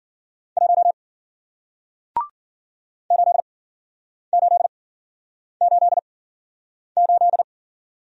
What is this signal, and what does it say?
Morse code, heard as a keyed tone, about 49 wpm, spaced as for 9 wpm, 700 Hz. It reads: 1 8888
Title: Speed-Racing Single Letter-Number 40wpm